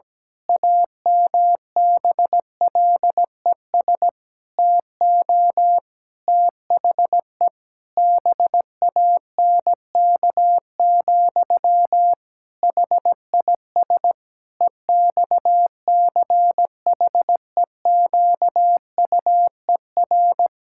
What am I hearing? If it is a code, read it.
Morse code, heard as a keyed tone, about 17 wpm, 700 Hz. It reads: AMBLES TO THE BANK, HIS EXCHEQUER
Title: Pangram Challenge v2 - 17wpm